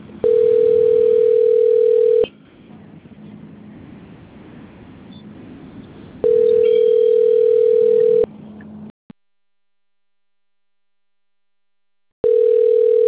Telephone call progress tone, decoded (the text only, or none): ringback tone